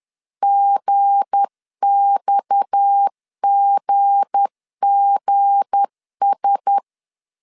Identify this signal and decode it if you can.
Morse code, heard as a keyed tone, about 10 wpm, 800 Hz. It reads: GXGGS
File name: LEZIONI / 7 GF9Z / 7 G 10wpm 2501